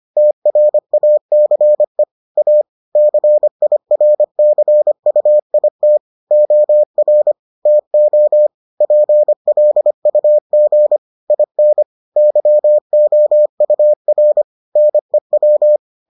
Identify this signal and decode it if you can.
Morse code, heard as a keyed tone, about 25 wpm, 600 Hz. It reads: TRACE A CIRCUIT OR TO PLUG IN YOUR NEW